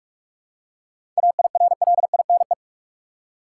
Morse code, heard as a keyed tone, about 46 wpm, 700 Hz. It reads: AIRLINE
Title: Seven Letter Words - 45wpm